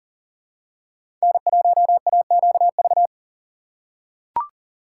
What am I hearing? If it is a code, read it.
Morse code, heard as a keyed tone, about 40 wpm, 700 Hz. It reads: N1AQV
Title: USA Call Signs - 4x Character Spacing - 40wpm